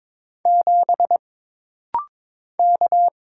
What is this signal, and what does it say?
Morse code, heard as a keyed tone, about 22 wpm, 700 Hz. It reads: 7 K
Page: This rapid-fire timing is geared to push your ICR (Instant Character Recognition) skills.